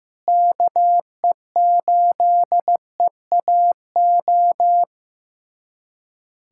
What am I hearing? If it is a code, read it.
Morse code, heard as a keyed tone, about 15 wpm, 700 Hz. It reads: KE8EAO
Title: USA Callsign DE Callsign 15wpm